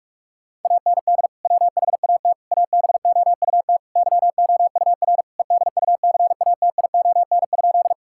Morse code, heard as a keyed tone, about 46 wpm, 700 Hz. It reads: AND WHAT ABOUT YOUR EDUCATION?
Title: Sentences from Top 1000 Words - Review Part 6 - 45wpm